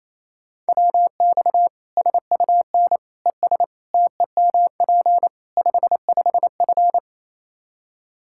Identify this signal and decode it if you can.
Morse code, heard as a keyed tone, about 28 wpm, 700 Hz. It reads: WX SUN ES TEMP 55F
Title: Practice QSO - Exchange 2 - Info - 28wpm